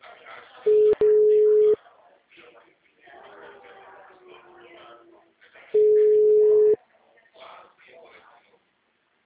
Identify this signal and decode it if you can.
Ringback tone (Europe, 425 Hz), with speech.